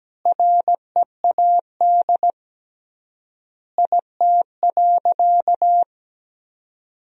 Morse code, heard as a keyed tone, about 17 wpm, 700 Hz. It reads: READ IT.
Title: 3x Word-Spacing - Sentences from Top 500 Words - 17wpm